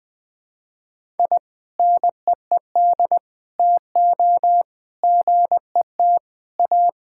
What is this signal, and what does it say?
Morse code, heard as a keyed tone, about 21 wpm, 700 Hz. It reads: I NEED TO GET A
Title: Sentences with 3-Word Phrases - 20wpm